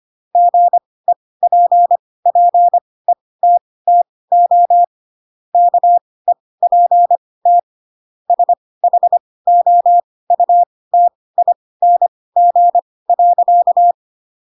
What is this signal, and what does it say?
Morse code, heard as a keyed tone, about 25 wpm, spaced as for 18 wpm, 700 Hz. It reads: GEPPETTO KEPT SHOUTING.
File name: Pinocchio ch3 12 25 ik1hge